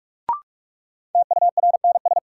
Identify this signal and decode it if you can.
Morse code, heard as a keyed tone, about 45 wpm, 700 Hz. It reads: TURNS
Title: Sets of 5 Words Encore - New Word Drill - 45wpm